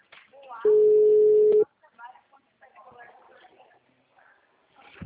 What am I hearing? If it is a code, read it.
Ringback tone (Europe, 425 Hz), with speech.